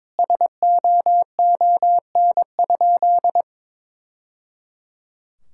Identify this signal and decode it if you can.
Morse code, heard as a keyed tone, about 22 wpm, 700 Hz. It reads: SOON?
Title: Sentences from Top 1000 Words - Review Part 6 - 22wpm